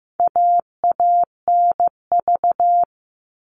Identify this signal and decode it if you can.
Morse code, heard as a keyed tone, about 15 wpm, 700 Hz. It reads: AANV
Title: All Cut Numbers - 4-Digit 15wpm